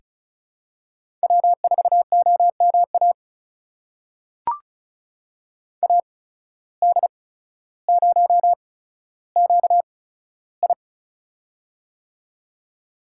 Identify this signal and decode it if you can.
Morse code, heard as a keyed tone, about 36 wpm, spaced as for 15 wpm, 700 Hz. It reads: W4OMA AD0QI
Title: USA Call Signs - 8x Character Spacing - 35wpm